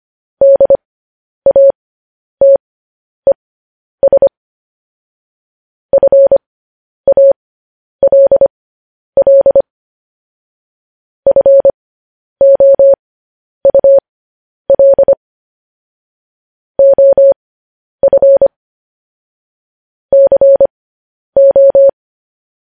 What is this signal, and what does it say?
Morse code, heard as a keyed tone, about 25 wpm, spaced as for 10 wpm, 550 Hz. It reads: DATES FALL FOUL OF CO
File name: podcast / News headlines-1x-25x10